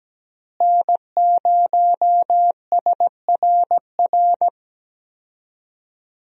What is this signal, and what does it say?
Morse code, heard as a keyed tone, about 17 wpm, 700 Hz. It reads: N0SRR
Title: USA Call Signs - 5 Characters - Repeated 2x - 17wpm